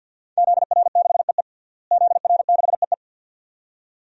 Morse code, heard as a keyed tone, about 52 wpm, 700 Hz. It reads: ZR6EE ZR6EE
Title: DXCC-100 Call Signs - 5 Characters - Repeated 2x - 50wpm